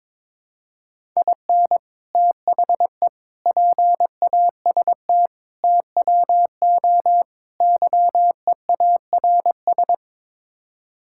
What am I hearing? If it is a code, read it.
Morse code, heard as a keyed tone, about 22 wpm, 700 Hz. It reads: IN THE PAST TWO YEARS
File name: practice / 22wpm / Build A Phrase - 5-Word Phrases - 22wpm